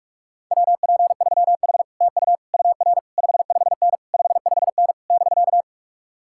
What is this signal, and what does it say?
Morse code, heard as a keyed tone, about 49 wpm, 700 Hz. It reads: WP3H TU UR 55N 55N <BK>